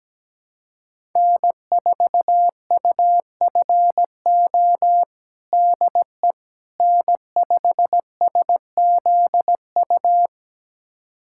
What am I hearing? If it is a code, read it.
Morse code, heard as a keyed tone, about 17 wpm, 700 Hz. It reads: N4UFO DE N5SZU
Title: USA Callsign DE Callsign - 5 Characters - 17wpm